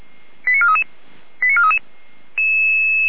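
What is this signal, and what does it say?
Five-tone selective call (ZVEI), heard as quick runs of five tones, tones of about 70 ms. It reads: 89433 89433
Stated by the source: ZVEI five-tone call 88074 received by SDR.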